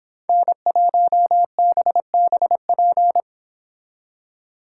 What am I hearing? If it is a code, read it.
Morse code, heard as a keyed tone, about 25 wpm, 700 Hz. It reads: N1BBP